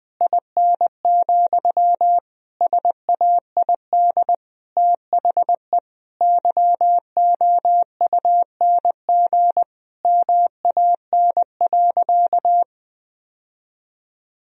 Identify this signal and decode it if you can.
Morse code, heard as a keyed tone, about 20 wpm, 700 Hz. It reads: IN, SAID THE YOUNG MAN.
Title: Sentences from Top 1000 Words - Review Part 2 - 20wpm